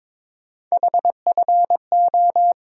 Morse code, heard as a keyed tone, about 22 wpm, 700 Hz. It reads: HFO
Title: USA Call Sign Postfixes - 3 Letters 22wpm